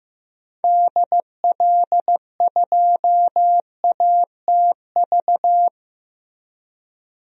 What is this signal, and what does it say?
Morse code, heard as a keyed tone, about 15 wpm, 700 Hz. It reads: DL2ATV
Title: DXCC-100 Call Signs - 6 Characters - Repeated 3x - 15wpm